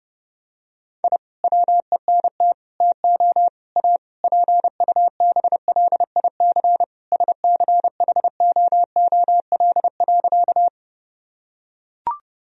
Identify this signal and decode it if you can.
Morse code, heard as a keyed tone, about 30 wpm, 700 Hz. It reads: I WENT TO A PUBLIC SCHOOL.
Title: Sentences from Top 1000 Words - Review Part 3 - 30wpm